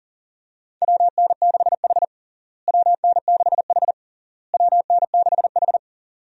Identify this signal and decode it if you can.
Morse code, heard as a keyed tone, about 37 wpm, 700 Hz. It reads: WN6H WN6H WN6H
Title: USA Call Signs - 4 Characters - Repeated 3x - 40wpm